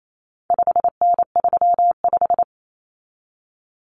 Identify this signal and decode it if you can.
Morse code, heard as a keyed tone, about 28 wpm, 700 Hz. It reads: 5N35
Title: Common Cut Numbers - 4-Digit 28wpm